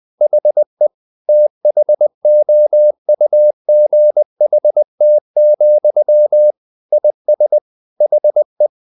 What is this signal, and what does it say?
Morse code, heard as a keyed tone, about 20 wpm, 600 Hz. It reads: HE THOUGHT, IS HE
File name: the old man and the sea part 2 20wpm